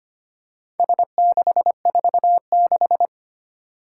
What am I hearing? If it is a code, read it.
Morse code, heard as a keyed tone, about 25 wpm, 700 Hz. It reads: S646
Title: Four Challenging Letter-Number Combinations 25wpm